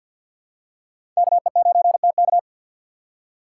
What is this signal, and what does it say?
Morse code, heard as a keyed tone, about 53 wpm, 700 Hz. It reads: KE9TK
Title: USA Call Signs - 5 Characters - 2x Character Spacing - 50wpm